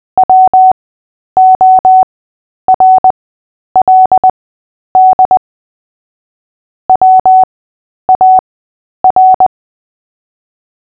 Morse code, heard as a keyed tone, about 20 wpm, spaced as for 10 wpm, 750 Hz. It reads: WORLD WAR
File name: Short-Story-7-10wpm